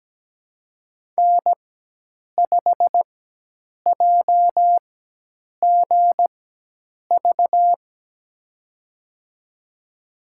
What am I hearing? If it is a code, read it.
Morse code, heard as a keyed tone, about 17 wpm, spaced as for 8 wpm, 700 Hz. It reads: N5JGV
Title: USA Call Signs - 5 Characters - 4x Character Spacing - 17wpm